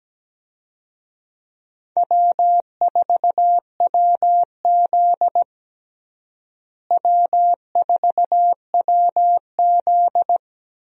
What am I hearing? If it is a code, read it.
Morse code, heard as a keyed tone, about 17 wpm, 700 Hz. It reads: W4WZ W4WZ
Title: USA Call Signs - 4 Characters - Repeated 3x - 17wpm